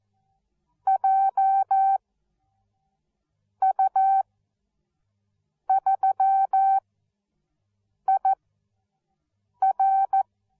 Morse code, heard as a keyed tone, about 14 wpm, spaced as for 5 wpm, 770 Hz. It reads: JU3IR
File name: Onlinelehrgang / morsen / DJ4UF Morsen13